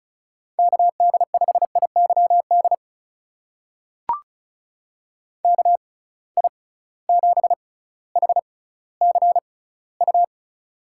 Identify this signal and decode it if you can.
Morse code, heard as a keyed tone, about 36 wpm, spaced as for 18 wpm, 700 Hz. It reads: KD5IYD KI7HCU
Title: USA Call Signs - 6 Characters - 6x Character Spacing - 35wpm